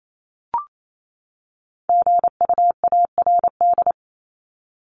Morse code, heard as a keyed tone, about 28 wpm, 700 Hz. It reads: GUARD
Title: Top 501-750 Words 28wpm